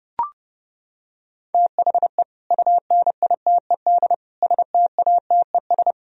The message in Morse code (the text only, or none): THE UNITED STATES